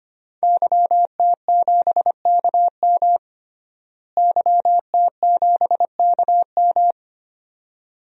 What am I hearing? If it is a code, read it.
Morse code, heard as a keyed tone, about 25 wpm, 700 Hz. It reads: YT7KM YT7KM
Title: DXCC-100 Call Signs - 5 Characters - Repeated 2x - 25wpm